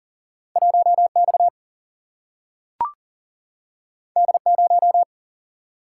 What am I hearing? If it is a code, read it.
Morse code, heard as a keyed tone, about 40 wpm, 700 Hz. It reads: 1X D0
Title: Single-Letter Single-Number 40wpm